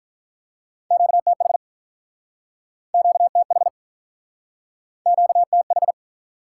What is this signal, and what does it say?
Morse code, heard as a keyed tone, about 48 wpm, 700 Hz. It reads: QTH QTH QTH